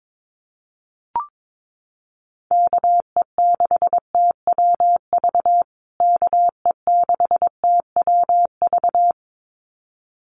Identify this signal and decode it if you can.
Morse code, heard as a keyed tone, about 22 wpm, 700 Hz. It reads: KE6TWV KE6TWV
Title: USA Call Signs Repeated 2x 22wpm